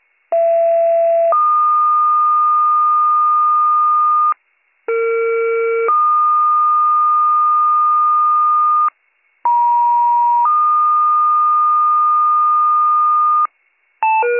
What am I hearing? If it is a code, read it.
Two-tone page, heard as a one-second tone, then a three-second tone, about 670 then 1220 Hz, about 460 then 1220 Hz, about 950 then 1220 Hz. It is heard from 0.3 to 13.5 s.